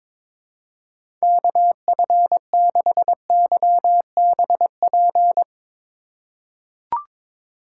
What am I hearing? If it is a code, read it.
Morse code, heard as a keyed tone, about 22 wpm, 700 Hz. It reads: KF6YBP
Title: USA Call Signs - 6 Characters - 2x Character Spacing - 22wpm